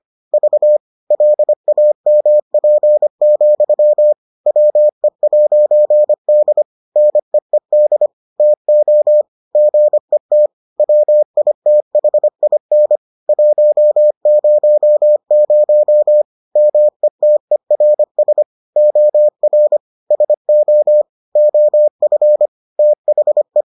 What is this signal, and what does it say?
Morse code, heard as a keyed tone, about 25 wpm, 600 Hz. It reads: V LAMP, WE'D NEED TO GET WITHIN 100 METERS OR SO OF THE